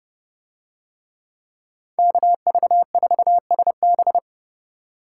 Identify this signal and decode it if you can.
Morse code, heard as a keyed tone, about 29 wpm, 700 Hz. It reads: KV4SB